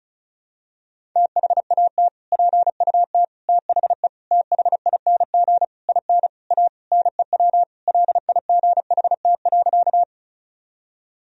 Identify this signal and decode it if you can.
Morse code, heard as a keyed tone, about 36 wpm, 700 Hz. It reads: THAT PUT THE THING IN A NEW LIGHT.
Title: Sentences from Top 1000 Words - Review Part 1 - 35wpm